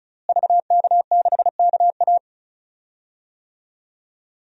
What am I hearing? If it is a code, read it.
Morse code, heard as a keyed tone, about 36 wpm, 700 Hz. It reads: VK6KA
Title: DXCC-100 Call Signs - 5 Characters - Repeated 2x - 35wpm